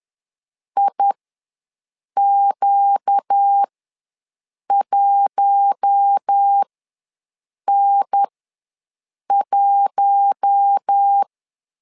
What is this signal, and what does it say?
Morse code, heard as a keyed tone, about 10 wpm, spaced as for 6 wpm, 800 Hz. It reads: IQ1N1
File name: LEZIONI / 4 Q1 ws 10wpm 2538